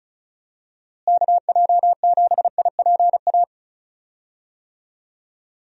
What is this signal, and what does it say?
Morse code, heard as a keyed tone, about 34 wpm, 700 Hz. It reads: KJ7IPA
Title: USA Call Signs - 6 Characters - Repeated 2x - 35wpm